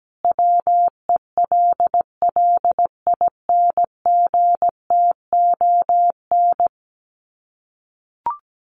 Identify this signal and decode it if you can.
Morse code, heard as a keyed tone, about 17 wpm, 700 Hz. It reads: WELLINGTON
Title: Common QSO Phrases 17wpm